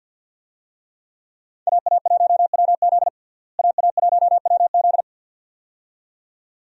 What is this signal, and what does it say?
Morse code, heard as a keyed tone, about 47 wpm, 700 Hz. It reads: AA1WZ AA1WZ
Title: USA Call Signs - 5 Characters - Repeated 2x - 50wpm